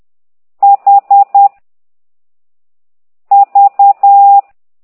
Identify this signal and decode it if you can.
Morse code, heard as a keyed tone, about 10 wpm, 800 Hz. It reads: H V
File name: ct2idw / exerc6